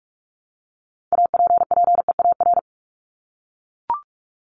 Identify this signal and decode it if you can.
Morse code, heard as a keyed tone, about 44 wpm, 700 Hz. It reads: APPEAR
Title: ED Suffix - Top 100 Words with Base Words 45wpm